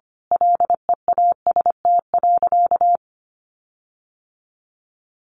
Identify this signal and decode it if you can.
Morse code, heard as a keyed tone, about 25 wpm, 700 Hz. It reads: LEAST.
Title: Sentences from Top 1300 Words 25wpm